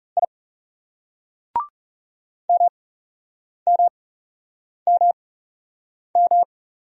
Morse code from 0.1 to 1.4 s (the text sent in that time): I